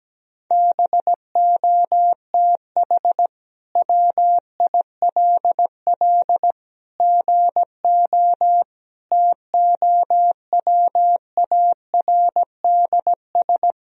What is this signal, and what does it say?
Morse code, heard as a keyed tone, about 17 wpm, 700 Hz. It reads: BOTH WILL GO TOWARDS
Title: Sentences from Top 1000 Words - Review Part 2 - 17wpm